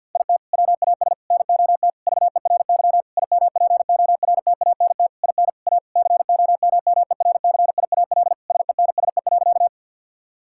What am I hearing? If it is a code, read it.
Morse code, heard as a keyed tone, about 50 wpm, 700 Hz. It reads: IT WAS NOT VERY IMPORTANT IN A COMMERCIAL SENSE.
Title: Sentences from Top 1000 Words - Review Part 5 - 50wpm